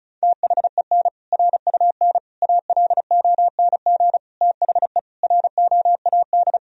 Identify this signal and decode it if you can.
Morse code, heard as a keyed tone, about 34 wpm, 700 Hz. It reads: THEN RUN ALONG THE ROAD
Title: Sentences from Top 1000 Words - Review Part 2 - 35wpm